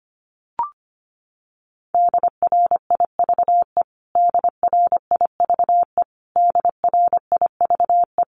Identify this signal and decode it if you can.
Morse code, heard as a keyed tone, about 25 wpm, 700 Hz. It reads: DRIVE DRIVE DRIVE
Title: Top 500 Words Repeated 25wpm